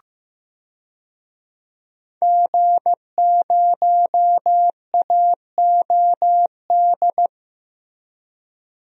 Morse code, heard as a keyed tone, about 15 wpm, 700 Hz. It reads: G0AOD
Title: DXCC-100 Call Signs - 5 Characters - Repeated 2x - 15wpm